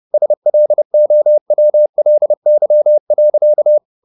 Morse code, heard as a keyed tone, about 30 wpm, 600 Hz. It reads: SLOWLY.
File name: the old man and the sea / the old man and the sea 1 30wpm